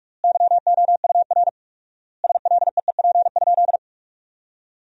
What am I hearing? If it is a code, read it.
Morse code, heard as a keyed tone, about 47 wpm, 700 Hz. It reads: YOUR SLEEP?
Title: 4x Word-Spacing - Sentences from Top 500 Words - 45wpm